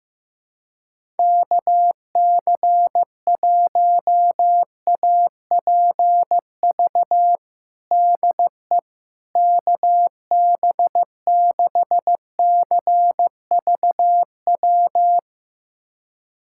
Morse code, heard as a keyed tone, about 15 wpm, 700 Hz. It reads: KC1APV DE KB6CVW
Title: USA Callsign DE Callsign 15wpm